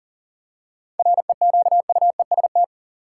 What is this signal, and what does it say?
Morse code, heard as a keyed tone, about 39 wpm, 700 Hz. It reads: REQUEST
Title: Seven Letter Words - 40wpm